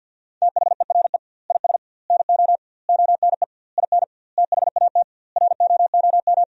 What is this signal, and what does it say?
Morse code, heard as a keyed tone, about 50 wpm, 700 Hz. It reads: THERE IS NO ONE IN THAT ROOM
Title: 2x Word-Spacing - Sentences from Top 500 Words - 50wpm